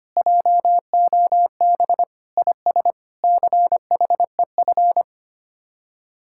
Morse code, heard as a keyed tone, about 32 wpm, 700 Hz. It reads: JOB IS CHEF